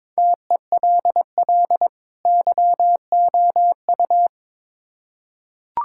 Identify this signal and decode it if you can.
Morse code, heard as a keyed tone, about 22 wpm, 700 Hz. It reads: TELL YOU
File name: Build A Phrase - 2-Word Phrases - 22wpm